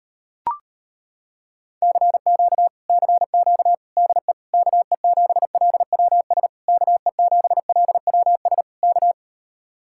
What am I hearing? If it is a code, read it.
Morse code, heard as a keyed tone, about 33 wpm, 700 Hz. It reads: CQ CQ DE KE7LWS KE7LWS K